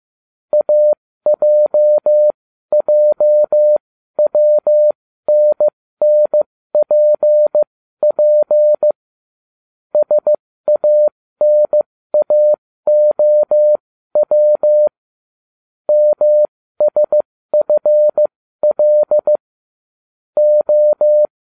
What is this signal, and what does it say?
Morse code, heard as a keyed tone, about 15 wpm, spaced as for 13 wpm, 600 Hz. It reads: AJJWNNPP SANAOW MSFL O